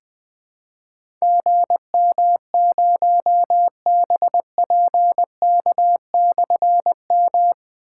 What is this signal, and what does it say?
Morse code, heard as a keyed tone, about 20 wpm, 700 Hz. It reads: GM0BPK/M
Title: Difficult Call Signs 20wpm